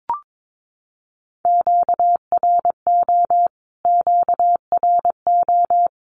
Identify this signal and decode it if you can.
Morse code, heard as a keyed tone, about 22 wpm, 700 Hz. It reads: QRO QRO
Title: Q-Codes Repeated 3x 22wpm